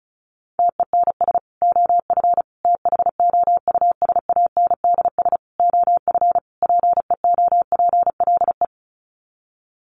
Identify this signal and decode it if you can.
Morse code, heard as a keyed tone, about 34 wpm, 700 Hz. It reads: TENS OF THOUSANDS OF PEOPLE
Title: Sets of 5 Words Encore2 35wpm